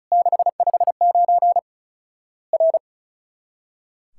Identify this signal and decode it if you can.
Morse code, heard as a keyed tone, about 34 wpm, 700 Hz. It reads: 659 R
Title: Worked All Europe DX Contest - QTC Exchange - 35wpm